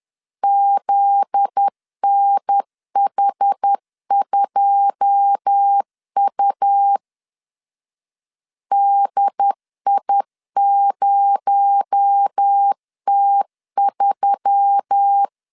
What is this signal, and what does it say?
Morse code, heard as a keyed tone, about 10 wpm, 800 Hz. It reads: ZNH2U DI0T3
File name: CW / DET / 1 9 1500C S A 10wpm 4511